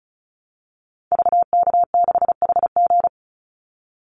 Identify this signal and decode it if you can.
Morse code, heard as a keyed tone, about 34 wpm, 700 Hz. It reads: VK6HG
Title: Call Signs - 35wpm